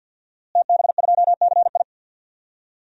Morse code, heard as a keyed tone, about 48 wpm, 700 Hz. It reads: TB2CI
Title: DXCC-100 Call Signs - 5 Characters - Repeated 2x - 50wpm